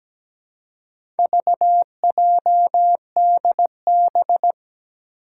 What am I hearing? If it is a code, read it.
Morse code, heard as a keyed tone, about 17 wpm, 700 Hz. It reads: VJDB
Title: Four Challenging Letter-Number Combinations 17wpm